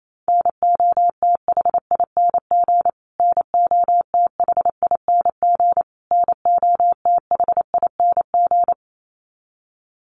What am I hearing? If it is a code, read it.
Morse code, heard as a keyed tone, about 28 wpm, 700 Hz. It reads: NOTHING NOTHING NOTHING
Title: Top 500 Words Repeated 28wpm